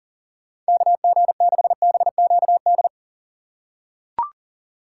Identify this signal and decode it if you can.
Morse code, heard as a keyed tone, about 40 wpm, 700 Hz. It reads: KG6BQD
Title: USA Call Signs - 4x Character Spacing - 40wpm